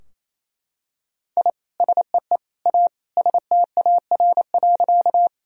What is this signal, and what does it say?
Morse code, heard as a keyed tone, about 28 wpm, 700 Hz. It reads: I SEE A STAR.
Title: Sentences from Top 1000 Words - Review Part 2 - 28wpm